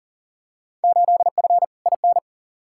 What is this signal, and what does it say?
Morse code, heard as a keyed tone, about 40 wpm, 700 Hz. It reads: 8F IN
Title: Field Day Exchanges Encore - 40wpm